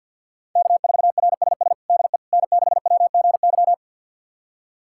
Morse code, heard as a keyed tone, about 50 wpm, 700 Hz. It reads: K4RSS DE N6WGY
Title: USA Callsign DE Callsign 50wpm